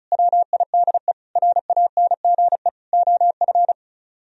Morse code, heard as a keyed tone, about 44 wpm, 700 Hz. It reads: WIDE RANGE OF
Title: Speed-Racing - Sets of 4 Words 35wpm